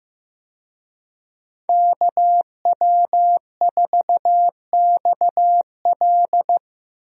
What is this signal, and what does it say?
Morse code, heard as a keyed tone, about 15 wpm, 700 Hz. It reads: KW4XL
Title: USA Call Signs - 5 Characters - Repeated 3x - 15wpm